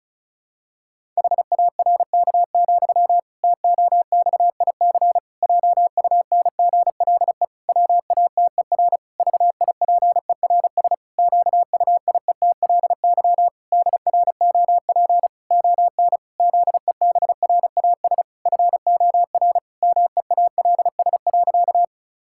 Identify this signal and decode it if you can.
Morse code, heard as a keyed tone, about 34 wpm, 700 Hz. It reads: HARK, TOXIC JUNGLE WATER VIPERS QUIETLY DROP ON ZEBRAS FOR MEALS.
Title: Pangram Challenge v2 - 35wpm